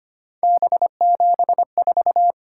Morse code, heard as a keyed tone, about 25 wpm, 700 Hz. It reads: B74
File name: Three Letter-Number Combinations - 25wpm